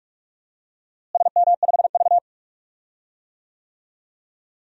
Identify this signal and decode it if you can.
Morse code, heard as a keyed tone, about 45 wpm, 700 Hz. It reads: SM5V
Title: DXCC-100 Call Signs - 4 Characters - Repeated 2x - 45wpm